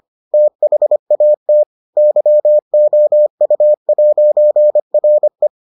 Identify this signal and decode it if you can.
Morse code, heard as a keyed tone, about 25 wpm, 600 Hz. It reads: THAT YOU'RE